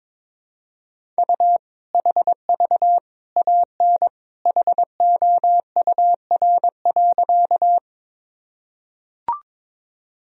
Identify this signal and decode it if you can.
Morse code, heard as a keyed tone, about 21 wpm, 700 Hz. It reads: U HV AN HOUR.
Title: Sentences from Top 400 Words and Abbreviations 22wpm